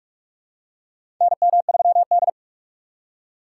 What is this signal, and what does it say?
Morse code, heard as a keyed tone, about 46 wpm, 700 Hz. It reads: NM3D
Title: USA Call Signs - 4 Characters - 6x Character Spacing - 45wpm